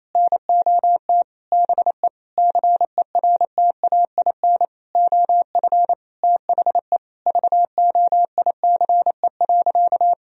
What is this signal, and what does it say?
Morse code, heard as a keyed tone, about 28 wpm, 700 Hz. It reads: NOT BE CERTAIN OF THE VOICE.
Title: Sentences from Top 1000 Words - Review Part 2 - 28wpm